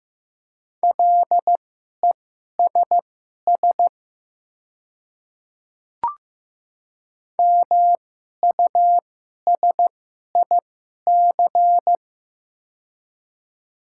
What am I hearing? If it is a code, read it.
Morse code, heard as a keyed tone, about 15 wpm, spaced as for 11 wpm, 690 Hz. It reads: LESS MUSIC
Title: 2x Character Spacing - Top 500 Words - 15wpm